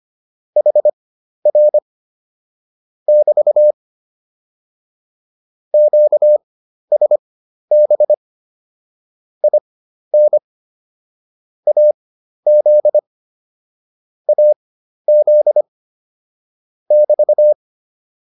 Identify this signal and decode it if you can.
Morse code, heard as a keyed tone, about 25 wpm, spaced as for 12 wpm, 600 Hz. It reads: HR <BT> QSB IN AZ AZ <BT>